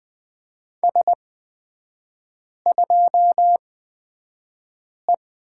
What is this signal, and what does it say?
Morse code, heard as a keyed tone, about 20 wpm, spaced as for 5 wpm, 700 Hz. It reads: S2E
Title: 236 Lesson 38e - Letter q - Three Characters 20wpm